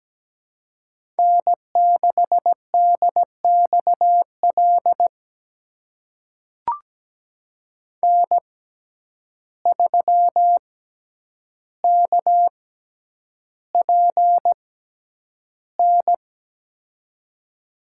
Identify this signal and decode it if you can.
Morse code, heard as a keyed tone, about 17 wpm, spaced as for 9 wpm, 700 Hz. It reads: N6DXL N3KPN
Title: USA Call Signs - 6x Character Spacing - 17wpm